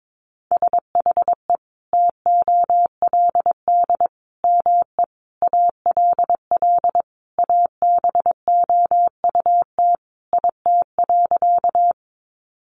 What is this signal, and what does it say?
Morse code, heard as a keyed tone, about 22 wpm, 700 Hz. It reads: SHE TOLD ME ALL ABOUT IT.
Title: Sentences from Top 400 Words 22wpm